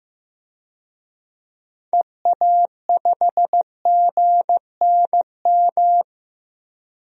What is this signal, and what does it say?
Morse code, heard as a keyed tone, about 15 wpm, 700 Hz. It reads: EA5GNM
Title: DXCC-100 Call Signs - 6 Characters - Repeated 2x - 15wpm